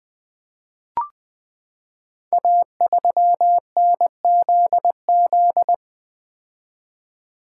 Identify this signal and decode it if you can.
Morse code, heard as a keyed tone, about 20 wpm, 700 Hz. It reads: A3NZZ
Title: Code Groups - 5 Letter:Numbers - 20wpm